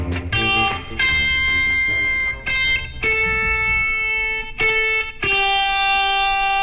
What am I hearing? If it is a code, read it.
Dial-up modem handshake.